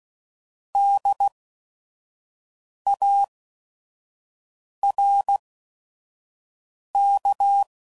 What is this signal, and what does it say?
Morse code, heard as a keyed tone, about 16 wpm, spaced as for 5 wpm, 800 Hz. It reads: DARK